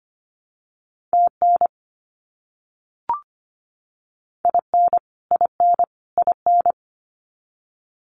Morse code, heard as a keyed tone, about 25 wpm, 700 Hz. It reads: TN IN IN IN
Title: US States Repeated 3x 25wpm